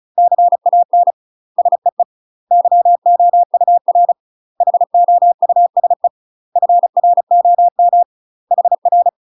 Morse code, heard as a keyed tone, about 34 wpm, 700 Hz. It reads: CAN SEE YOUR HOUSE FROM HR
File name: UK MASTER QSOs TRAINING / 35WPM / UK MASTER QSOs 0025